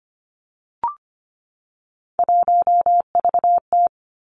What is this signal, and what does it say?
Morse code, heard as a keyed tone, about 25 wpm, 700 Hz. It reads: 1VT
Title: Three Challenging Letter-Number Combinations 25wpm